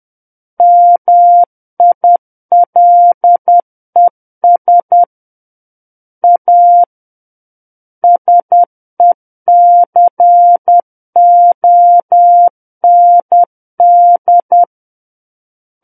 Morse code, heard as a keyed tone, about 10 wpm, 700 Hz. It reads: MILES A SECOND
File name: Book One / 01-Chapter 1 - 10 WPM